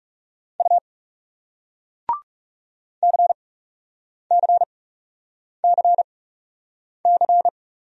Morse code, heard as a keyed tone, about 39 wpm, spaced as for 8 wpm, 700 Hz. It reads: U CCCC